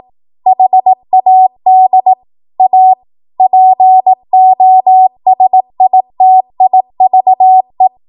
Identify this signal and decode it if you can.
Morse code, heard as a keyed tone, about 18 wpm, 750 Hz. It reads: HAD A POSITIVE
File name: Morse / 18WPM